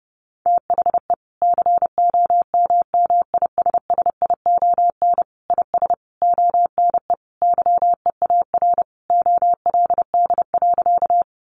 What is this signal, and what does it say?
Morse code, heard as a keyed tone, about 30 wpm, 700 Hz. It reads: THE COMMISSION IS ONE YEAR OLD.
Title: Sentences from Top 1400 Words 30wpm